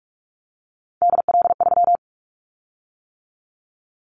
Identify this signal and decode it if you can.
Morse code, heard as a keyed tone, about 46 wpm, 700 Hz. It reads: DL3
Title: Callsign Prefixes for Top 25 DXCC - 45wpm